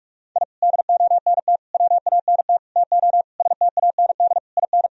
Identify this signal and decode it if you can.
Morse code, heard as a keyed tone, about 46 wpm, 700 Hz. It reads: I DONT WANT TO STAND IN
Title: Sentences from Top 1000 Words - Review Part 1 - 45wpm